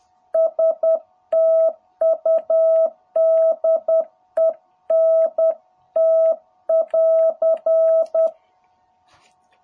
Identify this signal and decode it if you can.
Morse code, heard as a keyed tone, about 10 wpm, 660 Hz. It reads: STUDENT <AR>
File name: jh8cbh / 2021cw